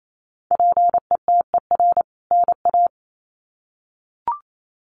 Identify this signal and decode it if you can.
Morse code, heard as a keyed tone, about 28 wpm, 700 Hz. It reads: PETER NA
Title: CWT Exchange 28wpm